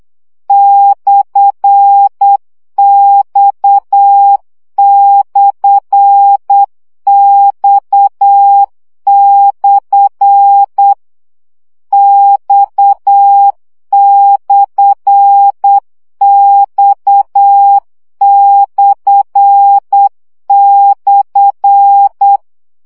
Morse code, heard as a keyed tone, about 8 wpm, 800 Hz. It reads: /X/X/ X/X//
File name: CW gyakorlas x per 7 baud